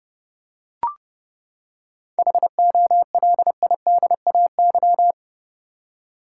Morse code, heard as a keyed tone, about 32 wpm, 700 Hz. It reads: HOLIDAY
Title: Seven Letter Words - 30wpm